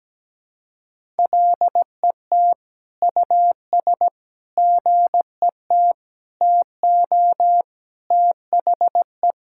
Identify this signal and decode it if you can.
Morse code, heard as a keyed tone, about 17 wpm, 700 Hz. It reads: LET US GET TO THE
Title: Sentences from Top 1600 Words 17wpm